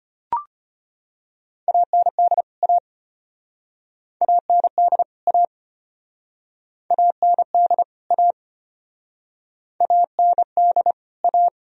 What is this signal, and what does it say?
Morse code, heard as a keyed tone, about 33 wpm, 700 Hz. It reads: AND A AND A AND A AND A